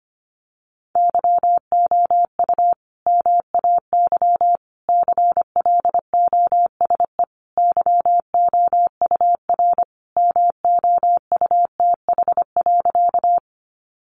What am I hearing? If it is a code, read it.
Morse code, heard as a keyed tone, about 25 wpm, 700 Hz. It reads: YOU MAY CLOSE YOUR MOUTH.
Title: Sentences from Top 1500 Words 25wpm